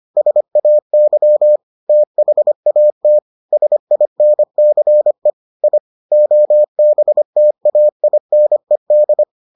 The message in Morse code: SAY THAT SINCE I OBTAINED

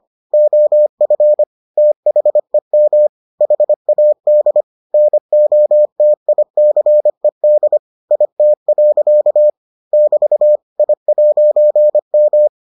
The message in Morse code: OF THEM HAD NOTICED IT. <BT> I'M

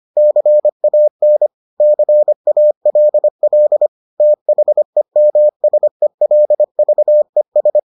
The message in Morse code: CAN CALL THEMSELVES

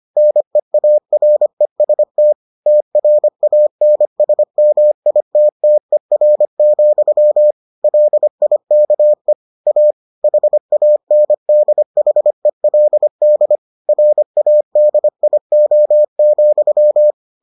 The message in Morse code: NEAREST TRANSMITTER, LIKE A HANDHELD RADIO,

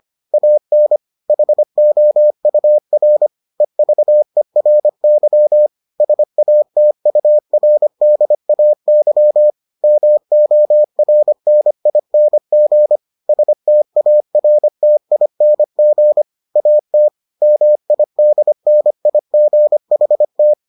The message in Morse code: AN HOUR EVERY SATURDAY MORNING STARTING AT MIDNIGHT

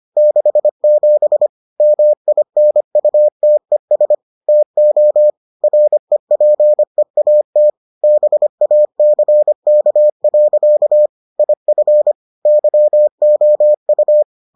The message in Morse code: 67 MINUTES TO REPEAT BACK. IF YOU

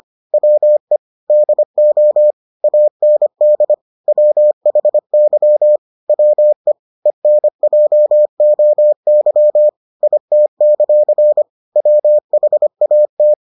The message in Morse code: WE DO AND WHY WE ENJOY IT; WHAT

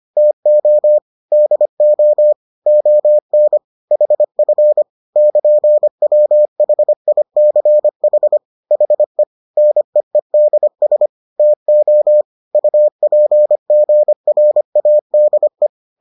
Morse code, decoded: TO DO ON HF <KN>WHICH HE NEEDS TO UPGRADE

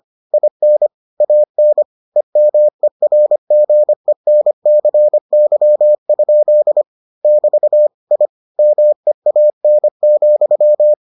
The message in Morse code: IN AN EMERGENCY? <BT> I MEAN,

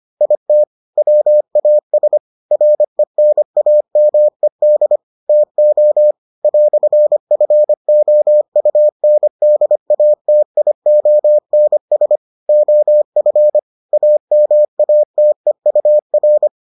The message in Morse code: IT WAS RENAMED TO "FOUNDATIONS OF AMATEUR

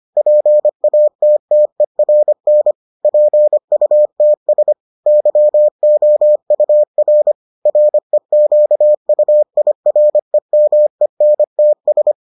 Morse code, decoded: PATTERN PUTS YOUR REQUIREMENTS

